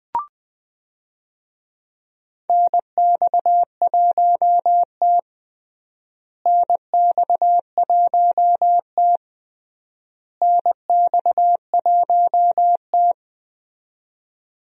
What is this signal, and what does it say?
Morse code, heard as a keyed tone, about 20 wpm, 700 Hz. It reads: NX1T NX1T NX1T
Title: USA Call Signs - 4 Characters - Repeated 3x - 20wpm